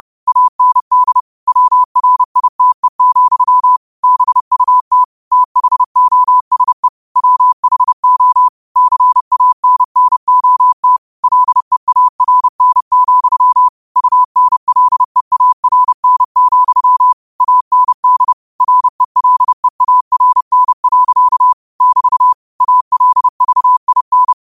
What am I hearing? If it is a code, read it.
Morse code, heard as a keyed tone, about 30 wpm, 1000 Hz. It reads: AND WRITE, BUT THOSE WHO CANNOT LEARN, UNLEARN, AND RELEARN. <BT> ALVIN
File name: qotdfiles / 30wpm